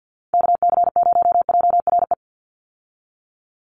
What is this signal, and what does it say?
Morse code, heard as a keyed tone, about 50 wpm, 700 Hz. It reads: KC0JRE
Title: USA Call Signs 50wpm